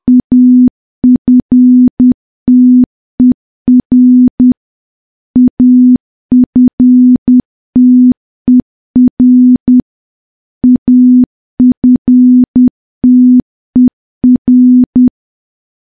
Morse code, heard as a keyed tone, about 10 wpm, 250 Hz. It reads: AFTER AFTER AFTER
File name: morse / morse-training-10WPM